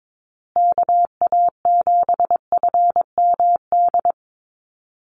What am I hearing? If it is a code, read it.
Morse code, heard as a keyed tone, about 22 wpm, 700 Hz. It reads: KA7FMD
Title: USA Call Signs Repeated 2x 22wpm